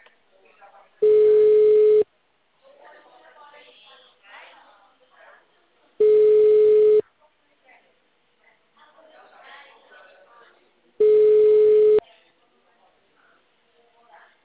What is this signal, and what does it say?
Ringback tone (Europe, 425 Hz), with speech.